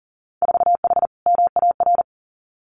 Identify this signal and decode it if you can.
Morse code, heard as a keyed tone, about 40 wpm, 700 Hz. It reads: 4H MAR
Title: Winter Field Day Exchanges Encore 40wpm